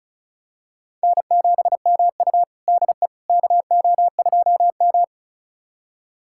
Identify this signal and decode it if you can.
Morse code, heard as a keyed tone, about 36 wpm, 700 Hz. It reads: N7MU DE KO2M
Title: USA Callsign DE Callsign - 4 Characters - 35wpm